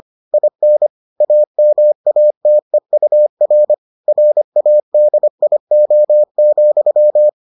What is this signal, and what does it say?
Morse code, heard as a keyed tone, about 25 wpm, 600 Hz. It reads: IN AMATEUR RADIO,